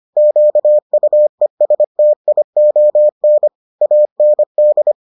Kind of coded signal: Morse code

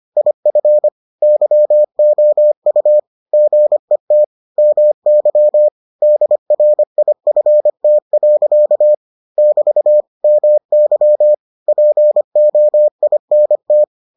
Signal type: Morse code

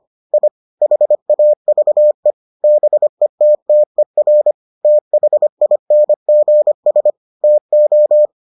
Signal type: Morse code

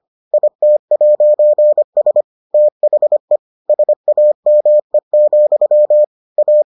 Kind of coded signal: Morse code

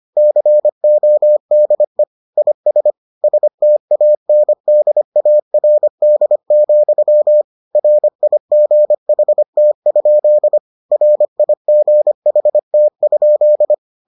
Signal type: Morse code

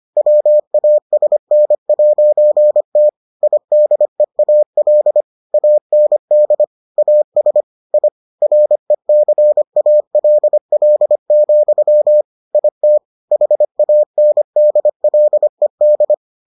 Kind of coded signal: Morse code

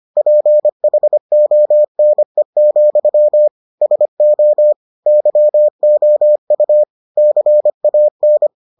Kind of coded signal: Morse code

